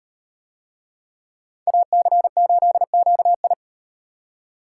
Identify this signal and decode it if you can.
Morse code, heard as a keyed tone, about 34 wpm, 700 Hz. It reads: AC8QI